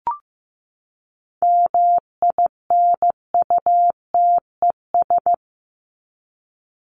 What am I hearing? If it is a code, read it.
Morse code, heard as a keyed tone, about 15 wpm, 700 Hz. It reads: MINUTES
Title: ES:S Suffix - Top 100 Words with Base Words 15wpm